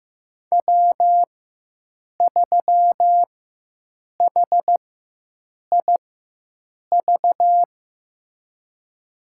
Morse code, heard as a keyed tone, about 16 wpm, spaced as for 7 wpm, 700 Hz. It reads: W3HIV
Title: USA Call Signs - 5 Characters - 4x Character Spacing - 15wpm